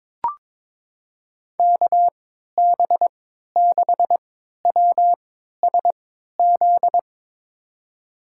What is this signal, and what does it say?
Morse code, heard as a keyed tone, about 21 wpm, spaced as for 12 wpm, 700 Hz. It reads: KB6WSZ
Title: USA Call Signs - 6 Characters - 3x Character Spacing - 22wpm